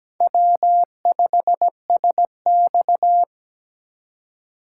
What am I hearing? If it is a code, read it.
Morse code, heard as a keyed tone, about 20 wpm, 700 Hz. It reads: W5SX
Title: Speed Racing - USA 1x2 Call Signs 17wpm